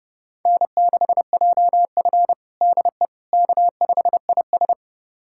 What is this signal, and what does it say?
Morse code, heard as a keyed tone, about 30 wpm, 700 Hz. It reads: N6JF DE K5IS
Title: USA Callsign DE Callsign - 4 Characters - 30wpm